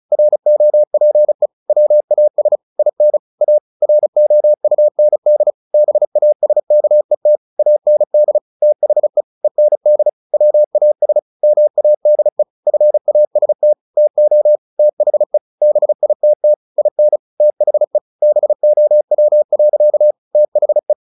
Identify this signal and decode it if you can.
Morse code, heard as a keyed tone, about 34 wpm, 600 Hz. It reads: ROPE WAS IN A ROUND BASKET AND THE END WAS MADE FAST TO THE BITT IN THE BOW. THE
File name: the old man and the sea 2 35wpm